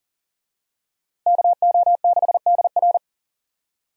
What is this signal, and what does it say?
Morse code, heard as a keyed tone, about 40 wpm, 700 Hz. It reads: KO6DR
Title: USA Call Signs - 5 Characters - Repeated 2x - 40wpm